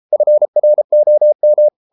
Morse code, heard as a keyed tone, about 33 wpm, 600 Hz. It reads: FROM